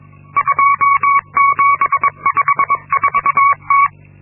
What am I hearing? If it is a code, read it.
Morse code, heard as a keyed tone, about 22 wpm, 1120 Hz. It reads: 2754T
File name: Linksmakalnis-2